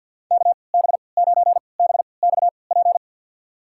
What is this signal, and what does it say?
Morse code, heard as a keyed tone, about 52 wpm, spaced as for 28 wpm, 700 Hz. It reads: KB9BXP
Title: USA Call Signs - 6 Characters - 3x Character Spacing - 50wpm